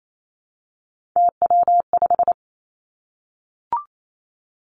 Morse code, heard as a keyed tone, about 28 wpm, 700 Hz. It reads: TW5
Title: Three Challenging Letter-Number Combinations 28wpm